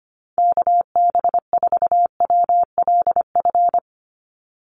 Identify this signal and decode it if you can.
Morse code, heard as a keyed tone, about 25 wpm, 700 Hz. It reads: KB4WLF
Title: Call Signs - 25wpm